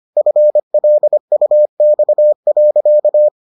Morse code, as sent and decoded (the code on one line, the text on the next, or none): ..-. .-.. ..- -..- .-.-.-
FLUX.